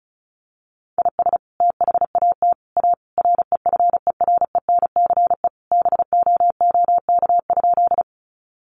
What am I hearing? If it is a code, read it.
Morse code, heard as a keyed tone, about 34 wpm, 700 Hz. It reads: IS THAT A REFERENCE BOOK?
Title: Sentences from Top 1200 Words 35wpm